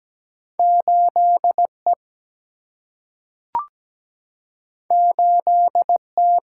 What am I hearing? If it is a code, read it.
Morse code, heard as a keyed tone, about 17 wpm, 700 Hz. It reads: 8E 8T
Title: Single-Letter Single-Number 17wpm